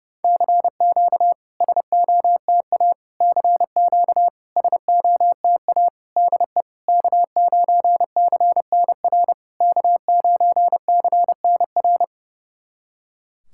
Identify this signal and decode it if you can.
Morse code, heard as a keyed tone, about 32 wpm, 690 Hz. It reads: CQ SOTA CQ SOTA DE K9CNR K9CNR
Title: SOTA QSO Practice 30wpm